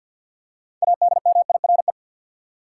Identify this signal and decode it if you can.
Morse code, heard as a keyed tone, about 47 wpm, 700 Hz. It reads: ADMIRE
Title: Six Letter Words - 50wpm